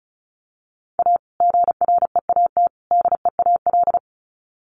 Morse code, heard as a keyed tone, about 34 wpm, 700 Hz. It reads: A GREAT DEAL
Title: Sets of 3 Words 35wpm